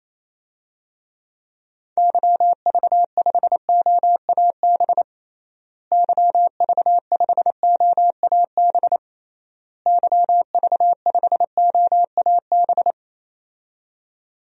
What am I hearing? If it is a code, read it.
Morse code, heard as a keyed tone, about 28 wpm, 700 Hz. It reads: YV5OAB YV5OAB YV5OAB
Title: DXCC-100 Call Signs - 6 Characters - Repeated 3x - 28wpm